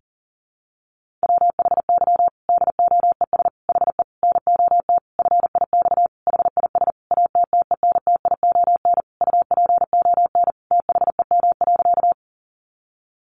Morse code, heard as a keyed tone, about 40 wpm, 700 Hz. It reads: WHY DOES HE NOT FIX HIS ATTENTION UPON THEM.
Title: Sentences from Top 1400 Words 40wpm